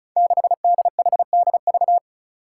Morse code, heard as a keyed tone, about 36 wpm, 700 Hz. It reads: 6DHDV
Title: Five Challenging Letter-Number Combinations 35wpm